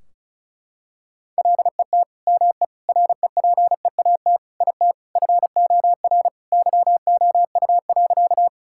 Morse code, heard as a keyed tone, about 36 wpm, 700 Hz. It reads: LET ME REPEAT IT FOR YOU.
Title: Sentences from Top 1600 Words 35wpm